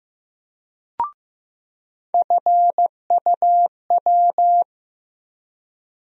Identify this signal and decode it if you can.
Morse code, heard as a keyed tone, about 15 wpm, 700 Hz. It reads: FUW
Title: USA Call Sign Postfixes - 3 Letters 15wpm